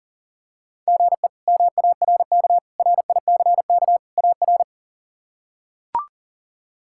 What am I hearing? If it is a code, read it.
Morse code, heard as a keyed tone, about 40 wpm, 700 Hz. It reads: GE MARK RICK AR
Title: K1USN SST QSO Practice - 40wpm